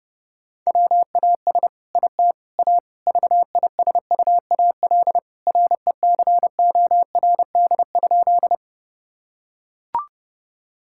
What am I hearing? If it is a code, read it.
Morse code, heard as a keyed tone, about 32 wpm, 700 Hz. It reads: WAS IT A VISUAL RECORD?
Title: Sentences from Top 1000 Words - Review Part 6 - 30wpm